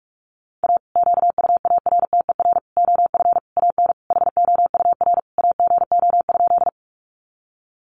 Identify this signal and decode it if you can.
Morse code, heard as a keyed tone, about 46 wpm, 700 Hz. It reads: A QUARTER OF AN HOUR AGO?
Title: Sentences from Top 1300 Words 45wpm